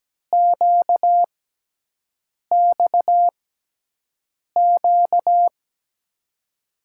Morse code, heard as a keyed tone, about 17 wpm, spaced as for 6 wpm, 700 Hz. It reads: QXQ
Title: USA Call Signs - 6x Character Spacing - 17wpm